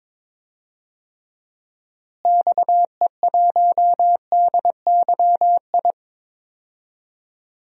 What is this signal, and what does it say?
Morse code, heard as a keyed tone, about 21 wpm, 700 Hz. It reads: XE1DYI
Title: DXCC-100 Call Signs - 6 Characters - Repeated 3x - 22wpm